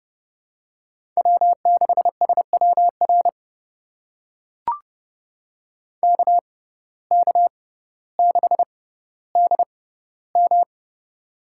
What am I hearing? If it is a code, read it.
Morse code, heard as a keyed tone, about 30 wpm, spaced as for 15 wpm, 700 Hz. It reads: W6SWR KK6DM
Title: USA Call Signs - 5 Characters - 6x Character Spacing - 30wpm